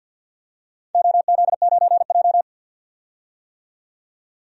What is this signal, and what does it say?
Morse code, heard as a keyed tone, about 46 wpm, 700 Hz. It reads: OZ9J
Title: DXCC-100 Call Signs - 4 Characters - 50wpm